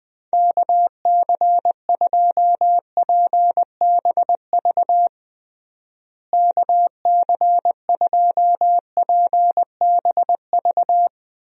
Morse code, heard as a keyed tone, about 20 wpm, 700 Hz. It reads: KC2PBV KC2PBV
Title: USA Call Signs - 6 Characters - Repeated 3x - 20wpm